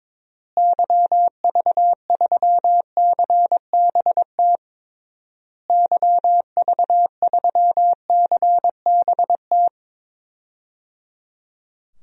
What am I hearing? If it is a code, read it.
Morse code, heard as a keyed tone, about 21 wpm, 700 Hz. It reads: YV3CBT YV3CBT
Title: DXCC-100 Call Signs - 6 Characters - Repeated 3x - 22wpm